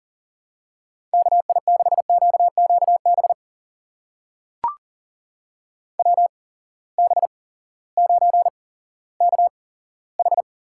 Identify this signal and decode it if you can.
Morse code, heard as a keyed tone, about 38 wpm, spaced as for 18 wpm, 700 Hz. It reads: KI6QQB WB9KH
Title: USA Call Signs - 6 Characters - 8x Character Spacing - 40wpm